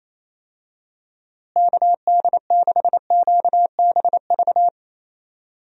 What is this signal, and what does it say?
Morse code, heard as a keyed tone, about 27 wpm, 700 Hz. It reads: KD6QBV
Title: USA Call Signs - 6 Characters - Repeated 2x - 28wpm